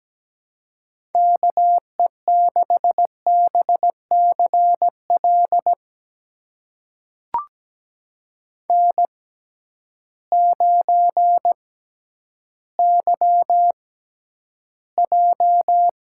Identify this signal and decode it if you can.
Morse code, heard as a keyed tone, about 17 wpm, spaced as for 10 wpm, 700 Hz. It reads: KE6BCL N9YJ
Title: USA Call Signs - 6x Character Spacing - 17wpm